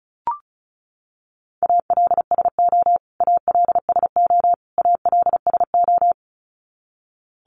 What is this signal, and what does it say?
Morse code, heard as a keyed tone, about 34 wpm, 700 Hz. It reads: ALSO ALSO ALSO
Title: Top 500 Words Repeated 35wpm